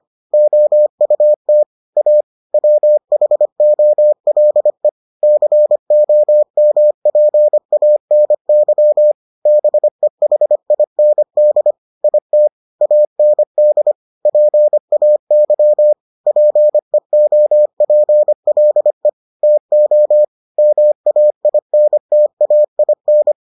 Morse code, heard as a keyed tone, about 25 wpm, 600 Hz. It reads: OUT A WHOLE COMPANY BEHIND IT AND PAY PEOPLE TO MAINTAIN